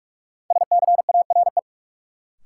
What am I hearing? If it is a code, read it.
Morse code, heard as a keyed tone, about 46 wpm, 700 Hz. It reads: SCARE
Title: Sentences from Top 1000 Words - Review Part 5 - New Word Drill - 45wpm